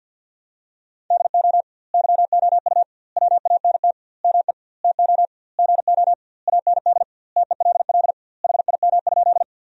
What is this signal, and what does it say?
Morse code, heard as a keyed tone, about 49 wpm, 700 Hz. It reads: DO YOU WANT ME TO GO AND TELL HIM?
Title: 2x Word-Spacing - Sentences from Top 200 Words - 50wpm